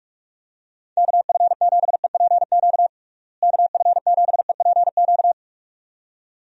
Morse code, heard as a keyed tone, about 46 wpm, 700 Hz. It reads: KF7EPQ KF7EPQ
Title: USA Call Signs - 6 Characters - Repeated 2x - 45wpm